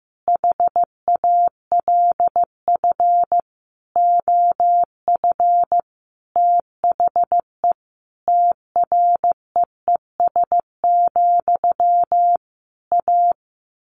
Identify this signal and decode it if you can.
Morse code, heard as a keyed tone, about 15 wpm, 700 Hz. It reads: HALF OF THE TREES, A
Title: A Sound Of Thunder 15wpm